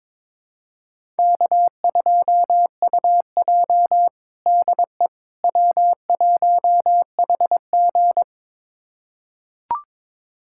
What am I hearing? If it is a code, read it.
Morse code, heard as a keyed tone, about 22 wpm, 700 Hz. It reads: K2UJ DE W1HG
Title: USA Callsign DE Callsign - 4 Characters - 22wpm